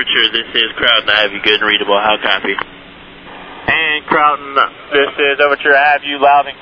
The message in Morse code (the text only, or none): none